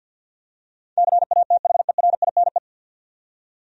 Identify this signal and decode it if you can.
Morse code, heard as a keyed tone, about 49 wpm, 700 Hz. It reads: CATHERINE
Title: USA Common Names - 50wpm